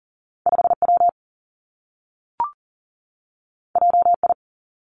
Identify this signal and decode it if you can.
Morse code, heard as a keyed tone, about 40 wpm, 700 Hz. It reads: 5 W J I
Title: Two Challenging Letter-Number Combinations 40wpm